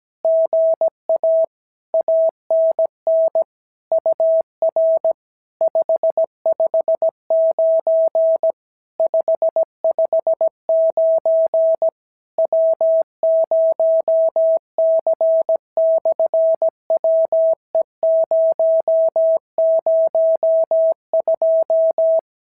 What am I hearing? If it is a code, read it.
Morse code, heard as a keyed tone, about 17 wpm, 650 Hz. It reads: GA ANN UR 559 559 W0C/WE002
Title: SOTA QSO Practice 17wpm